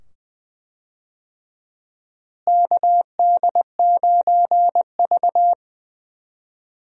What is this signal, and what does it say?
Morse code, heard as a keyed tone, about 20 wpm, 700 Hz. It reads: KD9V
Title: USA Call Signs - 4 Characters - Repeated 2x - 20wpm